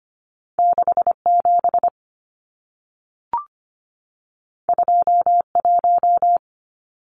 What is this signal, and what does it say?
Morse code, heard as a keyed tone, about 25 wpm, 700 Hz. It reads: 67 21
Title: Numbers - 2-Digit 25wpm